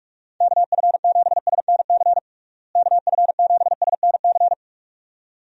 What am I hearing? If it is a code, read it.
Morse code, heard as a keyed tone, about 46 wpm, 700 Hz. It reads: KF7SNC KF7SNC
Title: USA Call Signs - 6 Characters - Repeated 2x - 45wpm